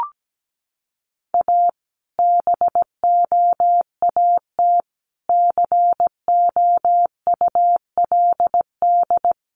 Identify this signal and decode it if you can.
Morse code, heard as a keyed tone, about 17 wpm, 700 Hz. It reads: A BOAT COULD
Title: Sentences from Top 1400 Words 17wpm